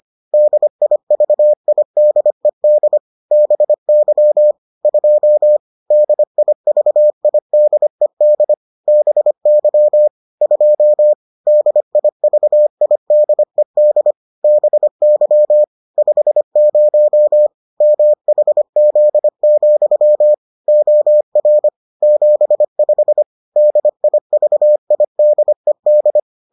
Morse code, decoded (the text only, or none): DIVIDED BY 2 DIVIDED BY 2 DIVIDED BY 50 MHZ, OR 75 DIVIDED